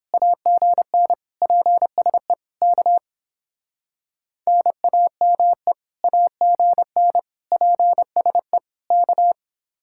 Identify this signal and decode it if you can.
Morse code, heard as a keyed tone, about 29 wpm, 700 Hz. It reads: AGN PSE K NAME AGN PSE K